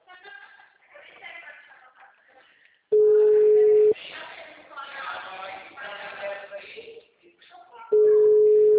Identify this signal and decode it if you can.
Ringback tone (Europe, 425 Hz), with speech.
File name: out- -2009- - - .55493